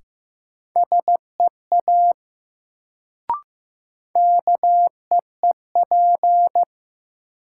Morse code, heard as a keyed tone, about 15 wpm, 700 Hz. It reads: SEA KEEP
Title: Sentences from Top 300 Words - New Word Drill - 15wpm